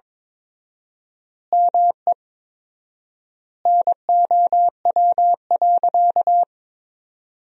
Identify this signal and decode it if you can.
Morse code, heard as a keyed tone, about 21 wpm, 700 Hz. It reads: ME NOW.
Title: 4x Word-Spacing - Sentences from Top 200 Words - 22wpm